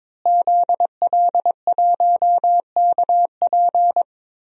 Morse code, heard as a keyed tone, about 22 wpm, 700 Hz. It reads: ZL1KP
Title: DXCC-100 Call Signs - 5 Characters - Repeated 3x - 22wpm